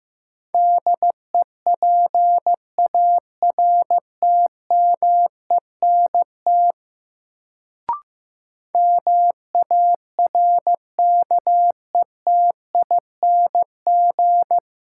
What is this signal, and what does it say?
Morse code, heard as a keyed tone, about 15 wpm, 700 Hz. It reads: DEPARTMENT MARKETING
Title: Sentences from Top 1000 Words - Review Part 5 - New Word Drill - 15wpm